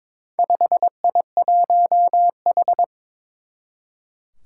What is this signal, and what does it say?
Morse code, heard as a keyed tone, about 22 wpm, 700 Hz. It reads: 5I1H
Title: Four Challenging Letter-Number Combinations 22wpm